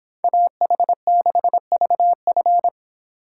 Morse code, heard as a keyed tone, about 28 wpm, 700 Hz. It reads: AH6VF